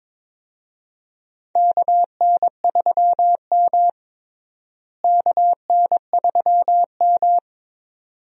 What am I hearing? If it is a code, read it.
Morse code, heard as a keyed tone, about 21 wpm, 700 Hz. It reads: KN3M KN3M
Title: USA Call Signs - 4 Characters - Repeated 2x - 22wpm